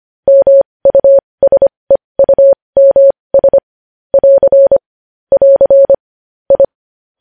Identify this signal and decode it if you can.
Morse code, heard as a keyed tone, about 25 wpm, spaced as for 20 wpm, 550 Hz. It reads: MUSEUMS <AR> <AR> I
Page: news-headlines-in-morse-code-at-20-wpm